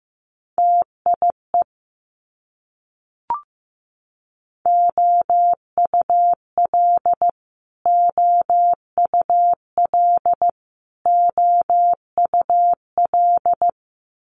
Morse code, heard as a keyed tone, about 15 wpm, 700 Hz. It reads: TIE OUL OUL OUL
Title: Multi-letter Phonemes Repeated 3x 15wpm